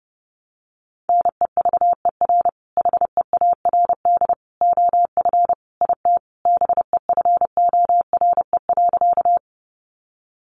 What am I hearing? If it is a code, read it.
Morse code, heard as a keyed tone, about 30 wpm, 700 Hz. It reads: NEVER HEARD OF IT BEFORE.
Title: Sentences from Top 400 Words 30wpm